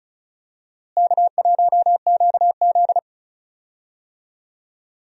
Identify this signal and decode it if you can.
Morse code, heard as a keyed tone, about 34 wpm, 700 Hz. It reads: K1QZ
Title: USA Call Signs - 4 Characters - Repeated 3x - 35wpm